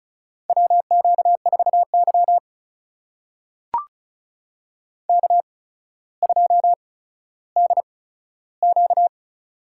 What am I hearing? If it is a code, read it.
Morse code, heard as a keyed tone, about 36 wpm, spaced as for 15 wpm, 700 Hz. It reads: WQ4Y K2DQ
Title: USA Call Signs - 4 Characters - 8x Character Spacing - 35wpm